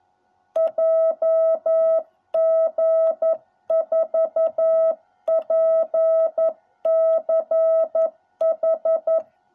Morse code, heard as a keyed tone, about 11 wpm, 660 Hz. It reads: JG4PCH